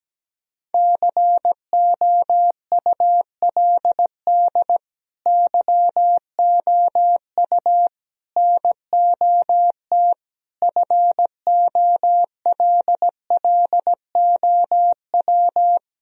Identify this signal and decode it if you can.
Morse code, heard as a keyed tone, about 17 wpm, 700 Hz. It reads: COULD YOU NOT FOLLOW
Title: Sentences from Top 1000 Words - Review Part 1 - 17wpm